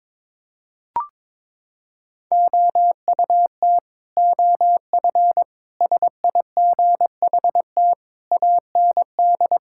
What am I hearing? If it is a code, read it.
Morse code, heard as a keyed tone, about 22 wpm, 700 Hz. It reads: OUT OF SIGHT AND
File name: practice / 22wpm / Sets of 7 Words - 22wpm